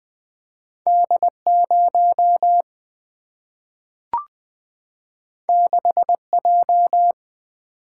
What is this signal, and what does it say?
Morse code, heard as a keyed tone, about 20 wpm, 700 Hz. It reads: D 0 6 J
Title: Single-Letter Single-Number 20wpm